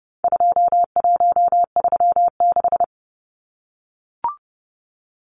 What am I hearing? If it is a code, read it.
Morse code, heard as a keyed tone, about 30 wpm, 700 Hz. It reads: 2136
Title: Numbers - 4-Digit 30wpm